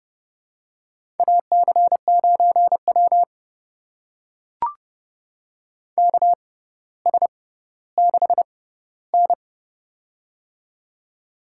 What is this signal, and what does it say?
Morse code, heard as a keyed tone, about 30 wpm, spaced as for 15 wpm, 700 Hz. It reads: AC9W KS6N
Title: USA Call Signs - 4 Characters - 6x Character Spacing - 30wpm